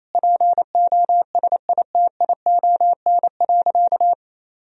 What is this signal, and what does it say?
Morse code, heard as a keyed tone, about 28 wpm, 700 Hz. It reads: POSITION.
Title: Sentences from Top 1000 Words - Review Part 2 - 28wpm